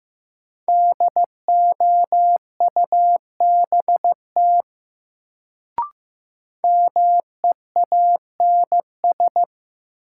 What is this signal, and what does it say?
Morse code, heard as a keyed tone, about 15 wpm, 700 Hz. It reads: DOUBT MEANS
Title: Sets of 5 Words - New Word Drill - 15wpm